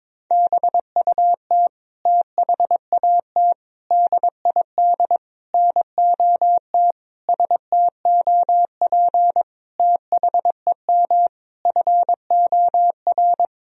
Morse code, heard as a keyed tone, about 22 wpm, 700 Hz. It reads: BUT THAT DID NOT STOP THEM FOR
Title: Sentences from Top 1000 Words - Review Part 1 - 22wpm